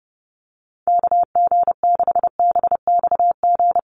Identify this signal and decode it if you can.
Morse code, heard as a keyed tone, about 30 wpm, 700 Hz. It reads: KG6BXG
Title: USA Call Signs 30wpm